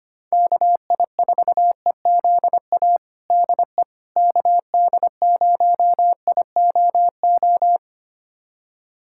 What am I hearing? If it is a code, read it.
Morse code, heard as a keyed tone, about 25 wpm, 700 Hz. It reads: KI4EZA DE KD0IOO
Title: USA Callsign DE Callsign - 6 Characters - 25wpm